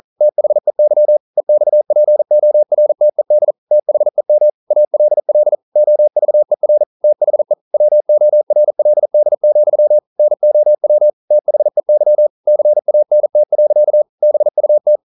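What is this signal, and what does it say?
Morse code, heard as a keyed tone, about 40 wpm, 600 Hz. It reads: THEY EXPORTED THEM ALL OVER THE WORLD, NOW THEY CANT. BUT
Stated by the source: Morse code practice at 41 wpm, from CW med Gnister.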